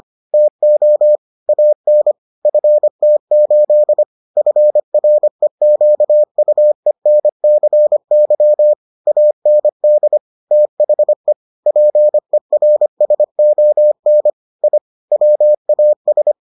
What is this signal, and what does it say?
Morse code, heard as a keyed tone, about 25 wpm, 600 Hz. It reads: TO AN FT8 FREQUENCY AND THE PERSON I WAS